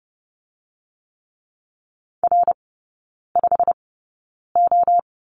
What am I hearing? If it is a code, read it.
Morse code, heard as a keyed tone, about 30 wpm, spaced as for 9 wpm, 700 Hz. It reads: R5O
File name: course / 30:10wpm / 087 Lesson 15e - Number 5 - Three Characters 30wpm